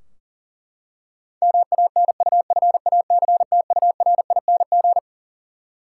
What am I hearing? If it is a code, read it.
Morse code, heard as a keyed tone, about 40 wpm, 700 Hz. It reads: MANUFACTURING
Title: K1USN SST QSO Practice 40wpm